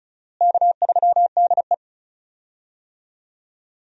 Morse code, heard as a keyed tone, about 36 wpm, 700 Hz. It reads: K3DE
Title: USA Call Signs - 4 Characters - Repeated 2x - 35wpm